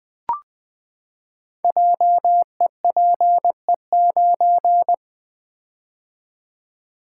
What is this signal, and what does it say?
Morse code, heard as a keyed tone, about 20 wpm, 700 Hz. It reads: JEPE9
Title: Code Groups - 5 Letter:Numbers - 20wpm